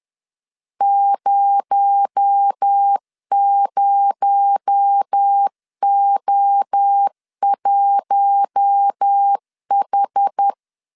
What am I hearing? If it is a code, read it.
Morse code, heard as a keyed tone, about 10 wpm, 800 Hz. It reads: 00O1H
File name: LEZIONI / 6 PKX0 / 6 PKX0 10wpm 2707